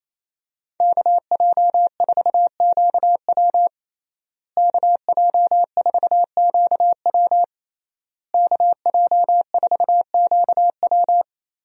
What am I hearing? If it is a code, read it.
Morse code, heard as a keyed tone, about 28 wpm, 700 Hz. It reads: KJ4QW KJ4QW KJ4QW
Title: USA Call Signs - 5 Characters - Repeated 3x - 28wpm